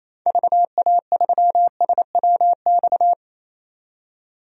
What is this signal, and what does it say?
Morse code, heard as a keyed tone, about 28 wpm, 700 Hz. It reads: VA3SWX
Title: DXCC-100 Call Signs - 6 Characters - 28wpm